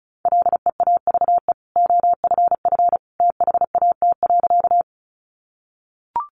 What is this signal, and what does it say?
Morse code, heard as a keyed tone, about 34 wpm, 700 Hz. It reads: LEAVE OFF THAT.
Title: Sentences from Top 400 Words 35wpm